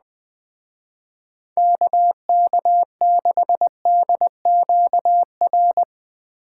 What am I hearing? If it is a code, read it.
Morse code, heard as a keyed tone, about 20 wpm, 700 Hz. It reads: KK6DQR